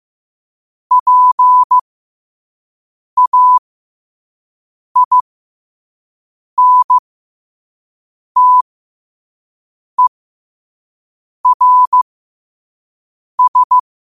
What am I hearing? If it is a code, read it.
Morse code, heard as a keyed tone, about 15 wpm, spaced as for 5 wpm, 1000 Hz. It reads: PAINTERS
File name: qotdfiles / 5wpm